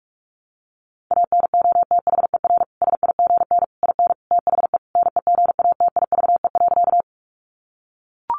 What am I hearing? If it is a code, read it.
Morse code, heard as a keyed tone, about 44 wpm, 700 Hz. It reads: ANOTHER SIGN IN THE NEGATIVE.
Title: Sentences from Top 1000 Words 45wpm